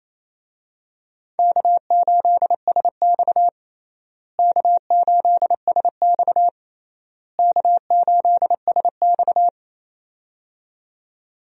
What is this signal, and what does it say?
Morse code, heard as a keyed tone, about 28 wpm, 700 Hz. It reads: K8SX K8SX K8SX
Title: USA Call Signs - 4 Characters - Repeated 3x - 28wpm